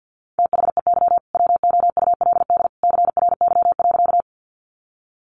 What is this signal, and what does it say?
Morse code, heard as a keyed tone, about 50 wpm, 700 Hz. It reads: THEY WOULD CRY.
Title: Sentences from Top 500 Words 50wpm